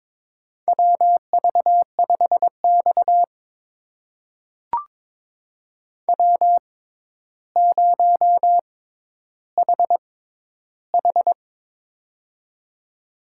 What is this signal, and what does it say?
Morse code, heard as a keyed tone, about 22 wpm, spaced as for 11 wpm, 700 Hz. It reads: WV5X W0HH
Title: USA Call Signs - 6x Character Spacing - 22wpm